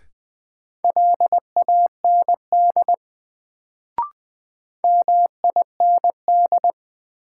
Morse code, heard as a keyed tone, about 20 wpm, 700 Hz. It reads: LAND MIND
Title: Sentences from Top 500 Words - New Word Drill - 20wpm